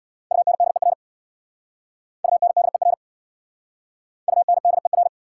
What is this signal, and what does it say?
Morse code, heard as a keyed tone, about 62 wpm, 690 Hz. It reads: UNDER UNDER UNDER
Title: Speed-Racing Prefixes and Suffixes 50wpm